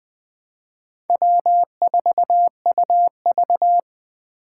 Morse code, heard as a keyed tone, about 20 wpm, 700 Hz. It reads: W4UV
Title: Speed Racing - USA 1x2 Call Signs 17wpm